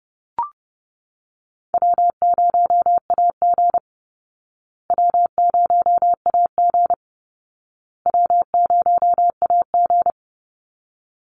Morse code, heard as a keyed tone, about 30 wpm, 700 Hz. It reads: W0AG W0AG W0AG
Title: Easy Call Signs Repeated 3x - 30wpm